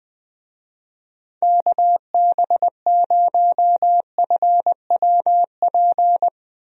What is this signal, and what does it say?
Morse code, heard as a keyed tone, about 21 wpm, 700 Hz. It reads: KB0FWP
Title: USA Call Signs - 6 Characters - 20wpm